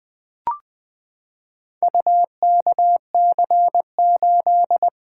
Morse code, heard as a keyed tone, about 20 wpm, 700 Hz. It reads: UKC8
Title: Four Letter-Number Combinations - 20wpm